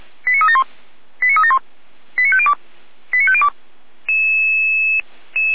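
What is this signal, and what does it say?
Five-tone selective call (ZVEI), heard as quick runs of five tones, tones of about 70 ms. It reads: 89371 89371 89502 89502
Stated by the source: ZVEI five-tone call 89249 received by SDR, from the Feuerwehr Orlamünde audio server.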